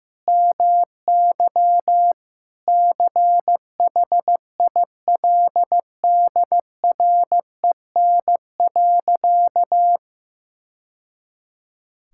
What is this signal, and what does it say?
Morse code, heard as a keyed tone, about 15 wpm, 700 Hz. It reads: MY CHILDREN.
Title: Sentences from Top 1000 Words - Review Part 5 - 15wpm